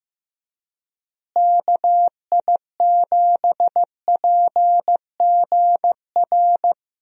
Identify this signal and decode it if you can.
Morse code, heard as a keyed tone, about 16 wpm, 700 Hz. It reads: KI7PGR
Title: USA Call Signs - 6 Characters - Repeated 3x - 15wpm